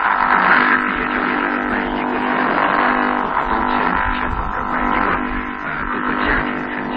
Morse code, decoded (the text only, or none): none